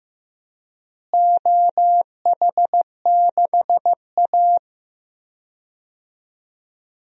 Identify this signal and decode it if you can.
Morse code, heard as a keyed tone, about 15 wpm, 700 Hz. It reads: OH6A
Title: DXCC-100 Call Signs - 4 Characters - Repeated 2x - 15wpm